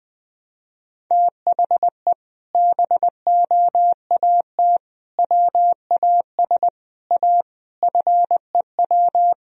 Morse code, heard as a keyed tone, about 21 wpm, 700 Hz. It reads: THE BOAT WAS A FEW
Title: Sentences from Top 1000 Words - Review Part 3 - 20wpm